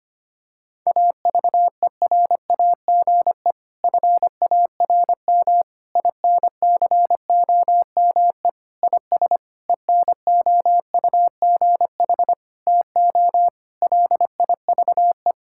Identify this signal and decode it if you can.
Morse code, heard as a keyed tone, about 25 wpm, 700 Hz. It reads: AVERAGE FARM INCOME IS ENOUGH TO LIVE
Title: Sentences from Top 1000 Words - Review Part 6 - 25wpm